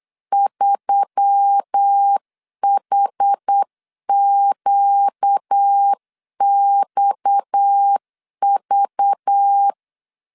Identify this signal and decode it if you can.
Morse code, heard as a keyed tone, about 8 wpm, 800 Hz. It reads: 3HQXV
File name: CW / DET / 1 6 1500C S C 8wpm 5219